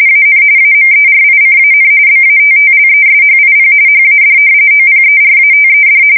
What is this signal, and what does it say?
RTTY, about 45.45 baud. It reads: AN AMERICAN BATTLESHIP HAD OPERATED
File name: fsk170 / battleshipnj